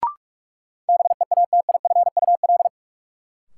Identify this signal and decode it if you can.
Morse code, heard as a keyed tone, about 46 wpm, 700 Hz. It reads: BEATIFUL
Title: Sentences from Top 1300 Words - New Word Drill Alt - 45wpm